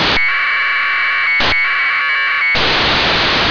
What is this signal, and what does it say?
Packet radio (AX.25, as in APRS).